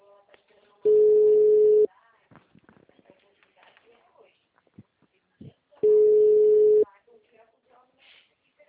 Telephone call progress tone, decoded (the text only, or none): ringback tone